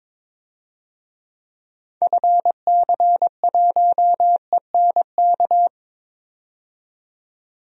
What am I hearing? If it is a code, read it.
Morse code, heard as a keyed tone, about 22 wpm, 700 Hz. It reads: FC1ENK
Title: DXCC-100 Call Signs - 6 Characters - 22wpm